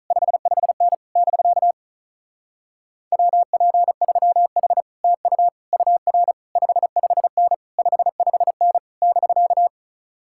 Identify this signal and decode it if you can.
Morse code, heard as a keyed tone, about 44 wpm, 700 Hz. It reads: 55N <BK> WP3H TU UR 55N 55N <BK>